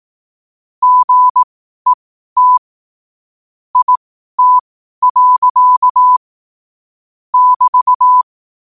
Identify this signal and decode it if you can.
Morse code, heard as a keyed tone, about 18 wpm, spaced as for 12 wpm, 1000 Hz. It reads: GET IT. <BT>
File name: qotdfiles / 12wpm